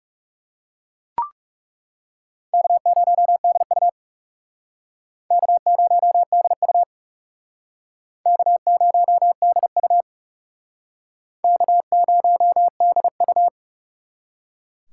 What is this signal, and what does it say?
Morse code, heard as a keyed tone, about 39 wpm, 700 Hz. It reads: K0DU K0DU K0DU K0DU